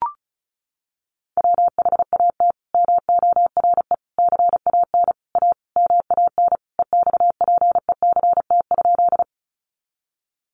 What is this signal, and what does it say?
Morse code, heard as a keyed tone, about 34 wpm, 700 Hz. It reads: WHAT MORE CAN A MAN EXPECT?
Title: Sentences from Top 1100 Words 35wpm